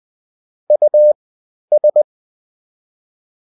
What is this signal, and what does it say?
Morse code, heard as a keyed tone, about 20 wpm, spaced as for 11 wpm, 600 Hz. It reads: US
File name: Top-100-Words-10